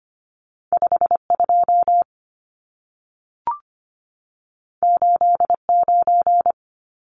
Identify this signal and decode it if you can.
Morse code, heard as a keyed tone, about 25 wpm, 700 Hz. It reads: 52 89
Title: Numbers - 2-Digit 25wpm